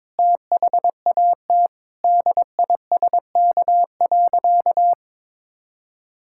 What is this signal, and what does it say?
Morse code, heard as a keyed tone, about 22 wpm, 700 Hz. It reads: THAT DISK.
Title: Sentences from Top 1000 Words - Review Part 6 - 22wpm